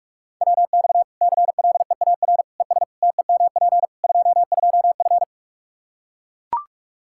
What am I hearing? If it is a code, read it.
Morse code, heard as a keyed tone, about 42 wpm, 700 Hz. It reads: WX CLEAR ES TEMP 22F
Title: Practice QSO - Exchange 2 - Info - 45wpm